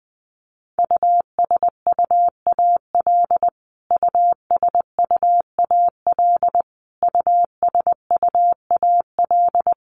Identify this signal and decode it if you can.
Morse code, heard as a keyed tone, about 20 wpm, 700 Hz. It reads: USUAL USUAL USUAL
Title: Top 500 Words Repeated 20wpm